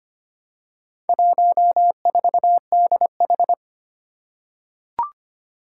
Morse code, heard as a keyed tone, about 25 wpm, 700 Hz. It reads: 14DH
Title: Four Challenging Letter-Number Combinations 25wpm